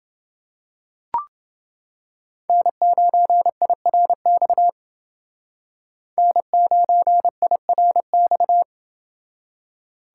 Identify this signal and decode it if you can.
Morse code, heard as a keyed tone, about 27 wpm, 700 Hz. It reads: N9IRX N9IRX